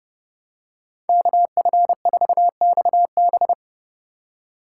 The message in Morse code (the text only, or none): KF4XB